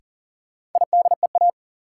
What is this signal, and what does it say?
Morse code, heard as a keyed tone, about 40 wpm, 700 Hz. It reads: IDEA
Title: Sentences from Top 300 Words - New Word Drill - 40wpm